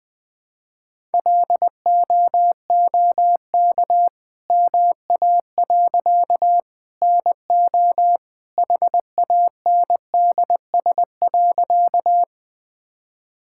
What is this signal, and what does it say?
Morse code, heard as a keyed tone, about 20 wpm, 700 Hz. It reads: LOOK MA. NO HANDS.
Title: All-Age Appropriate Jokes - 20wpm